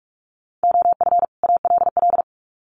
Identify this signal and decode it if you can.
Morse code, heard as a keyed tone, about 44 wpm, 700 Hz. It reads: OF ALL
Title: Sets of 2 Words 45wpm